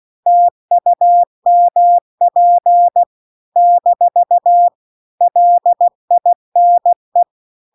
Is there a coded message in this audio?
Yes: Morse code from 0.3 to 7.2 s, heard as a keyed tone, about 16 wpm, 700 Hz. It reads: TUMP - LINE